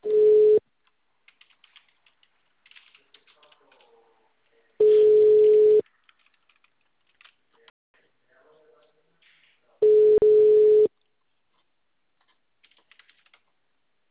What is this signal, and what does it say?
Ringback tone (Europe, 425 Hz), with speech.